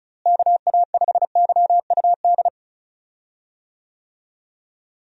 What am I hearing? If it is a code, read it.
Morse code, heard as a keyed tone, about 36 wpm, 700 Hz. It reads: KA5YUD
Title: USA Call Signs - 6 Characters - Repeated 2x - 35wpm